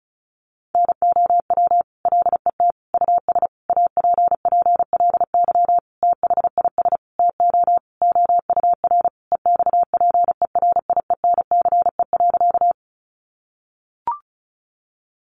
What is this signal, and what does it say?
Morse code, heard as a keyed tone, about 34 wpm, 700 Hz. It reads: NOW LET US APPLY THIS TO OUR EXPERIENCE.
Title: Sentences from Top 1200 Words 35wpm